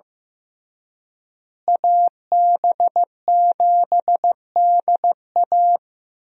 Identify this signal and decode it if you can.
Morse code, heard as a keyed tone, about 15 wpm, 700 Hz. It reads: AB7DA
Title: USA Call Signs - 5 Characters - Repeated 3x - 15wpm